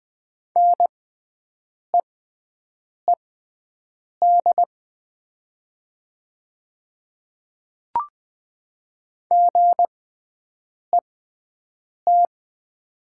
Morse code, heard as a keyed tone, about 20 wpm, spaced as for 7 wpm, 700 Hz. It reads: NEED GET
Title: 6x Character Spacing - Top 200 Words - 20wpm